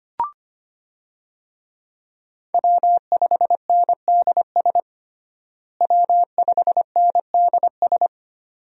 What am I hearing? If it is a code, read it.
Morse code, heard as a keyed tone, about 25 wpm, 700 Hz. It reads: W5NDS W5NDS
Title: USA Call Signs - 5 Characters - Repeated 2x - 25wpm